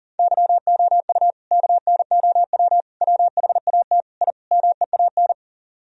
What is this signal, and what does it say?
Morse code, heard as a keyed tone, about 53 wpm, spaced as for 47 wpm, 700 Hz. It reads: YOU KNOW WHAT I MEAN
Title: Speed-Racing - Sets of 5 Words 40wpm